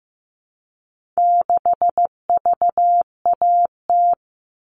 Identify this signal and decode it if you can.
Morse code, heard as a keyed tone, about 15 wpm, 700 Hz. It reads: 6VAT
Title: All Cut Numbers - 4-Digit 15wpm